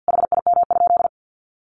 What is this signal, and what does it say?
Morse code, heard as a keyed tone, about 50 wpm, 700 Hz. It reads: HIM?
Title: Sentences from Top 1500 Words 50wpm